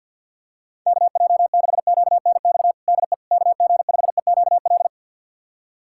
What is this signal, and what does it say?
Morse code, heard as a keyed tone, about 53 wpm, 700 Hz. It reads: KJ6QNX DE KG5EQL
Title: USA Callsign DE Callsign - 6 Characters - 50wpm